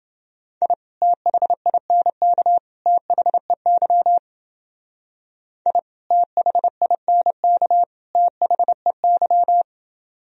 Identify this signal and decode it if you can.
Morse code, heard as a keyed tone, about 29 wpm, 700 Hz. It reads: I THINK THEY I THINK THEY